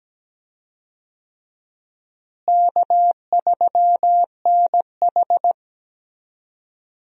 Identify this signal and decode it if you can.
Morse code, heard as a keyed tone, about 17 wpm, 700 Hz. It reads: K3NH
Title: USA Call Signs - 4 Characters - Repeated 3x - 17wpm